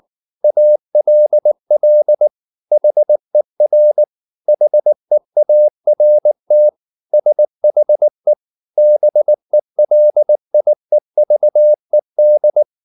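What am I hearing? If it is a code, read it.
Morse code, heard as a keyed tone, about 19 wpm, 600 Hz. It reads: ALL HER HEART SHE BELIEVED
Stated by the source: Morse code at 19 wpm from CW med Gnister, reading Ken Follett's novel Never in English.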